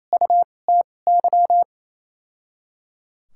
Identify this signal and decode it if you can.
Morse code, heard as a keyed tone, about 28 wpm, spaced as for 20 wpm, 700 Hz. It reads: UTY
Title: 2x Character Spacing - Top 500 Words - 28wpm